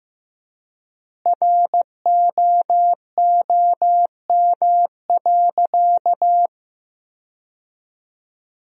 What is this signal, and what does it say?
Morse code, heard as a keyed tone, about 15 wpm, 700 Hz. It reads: ROOM.
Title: 3x Word-Spacing - Sentences from Top 400 Words - 15wpm